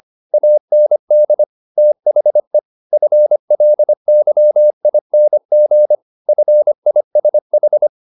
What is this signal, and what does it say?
Morse code, heard as a keyed tone, about 25 wpm, 600 Hz. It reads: AND THE FLYING FISH
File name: the old man and the sea part 1 25wpm